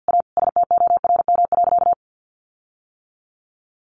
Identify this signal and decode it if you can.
Morse code, heard as a keyed tone, about 50 wpm, 700 Hz. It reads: A STORM.
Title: Sentences from Top 1500 Words 50wpm